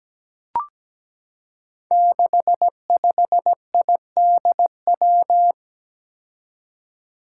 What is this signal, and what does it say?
Morse code, heard as a keyed tone, about 17 wpm, 700 Hz. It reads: 65IDW
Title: Five Challenging Letter-Number Combinations 17wpm